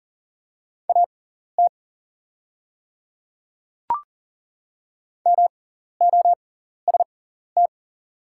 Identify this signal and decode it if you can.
Morse code, heard as a keyed tone, about 39 wpm, spaced as for 14 wpm, 700 Hz. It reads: AT MOST
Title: 6x Character Spacing - Top 300 Words - 40wpm